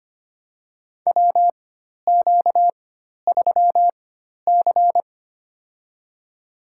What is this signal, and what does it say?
Morse code, heard as a keyed tone, about 25 wpm, spaced as for 12 wpm, 700 Hz. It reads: WQ3C
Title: USA Call Signs - 4 Characters - 4x Character Spacing - 25wpm